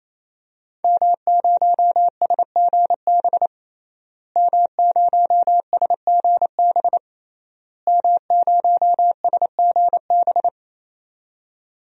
Morse code, heard as a keyed tone, about 28 wpm, 700 Hz. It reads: M0SGB M0SGB M0SGB
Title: DXCC-100 Call Signs - 5 Characters - Repeated 3x - 28wpm